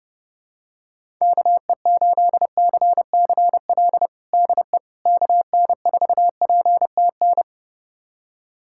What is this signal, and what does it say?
Morse code, heard as a keyed tone, about 30 wpm, 700 Hz. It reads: KE8CCL DE KN4PTN
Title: USA Callsign DE Callsign - 6 Characters - 30wpm